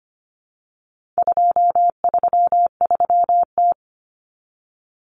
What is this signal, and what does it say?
Morse code, heard as a keyed tone, about 25 wpm, 700 Hz. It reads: 233T
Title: Common Cut Numbers - 4-Digit 25wpm